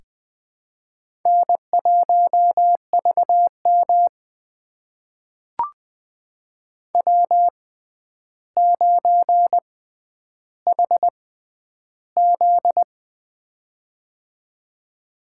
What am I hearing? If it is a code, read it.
Morse code, heard as a keyed tone, about 20 wpm, spaced as for 10 wpm, 700 Hz. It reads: N1VM W9HZ
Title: USA Call Signs - 4 Characters - 6x Character Spacing - 20wpm